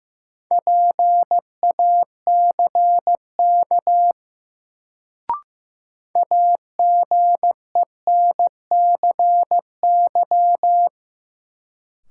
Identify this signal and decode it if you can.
Morse code, heard as a keyed tone, about 15 wpm, 700 Hz. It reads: PACK AGENCY
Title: Sentences from Top 1300 Words - New Word Drill Alt - 15wpm